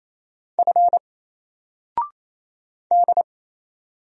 Morse code, heard as a keyed tone, about 27 wpm, 700 Hz. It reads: F D
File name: Single-Letter-Number-28wpm